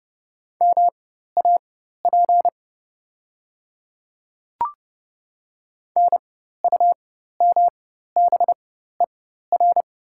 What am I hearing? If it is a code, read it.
Morse code, heard as a keyed tone, about 32 wpm, spaced as for 14 wpm, 700 Hz. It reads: MAP NUMBER
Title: 4x Character Spacing - Top 500 Words - 30wpm